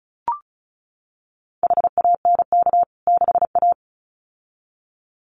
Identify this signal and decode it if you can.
Morse code, heard as a keyed tone, about 34 wpm, 700 Hz. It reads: HANK 6A
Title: CWT Exchange 35wpm